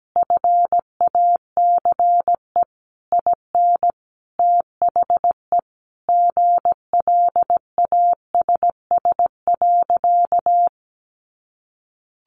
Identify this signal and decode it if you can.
Morse code, heard as a keyed tone, about 17 wpm, 700 Hz. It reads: FACE IN THE GLASS.
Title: Sentences from Top 1300 Words 17wpm